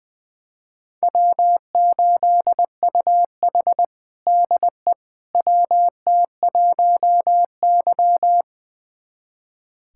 Morse code, heard as a keyed tone, about 21 wpm, 700 Hz. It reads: W8UH DE WT1Y
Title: USA Callsign DE Callsign - 4 Characters - 20wpm